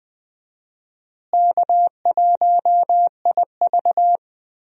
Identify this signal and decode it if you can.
Morse code, heard as a keyed tone, about 20 wpm, 700 Hz. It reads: K1IV
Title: USA Call Signs - 4 Characters - Repeated 3x - 20wpm